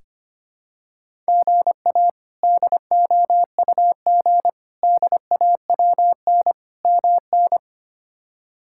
Morse code, heard as a keyed tone, about 25 wpm, 700 Hz. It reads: GA DOUG DAWN MN
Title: K1USN SST QSO Practice - 25wpm